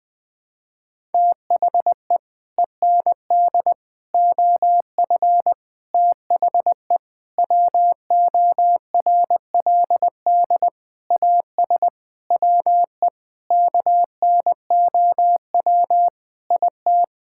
Morse code, heard as a keyed tone, about 20 wpm, 700 Hz. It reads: THE END OF THE WORLD AS WE KNOW IT
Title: Sets of 9 Words - 20wpm